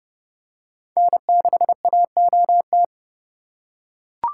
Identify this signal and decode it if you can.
Morse code, heard as a keyed tone, about 32 wpm, 700 Hz. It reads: N6AOT
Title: USA Call Signs - 5 Characters - 3x Character Spacing - 30wpm